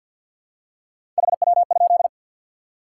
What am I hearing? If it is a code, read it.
Morse code, heard as a keyed tone, about 53 wpm, 700 Hz. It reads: HW?
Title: Top 90 QSO Elements 50wpm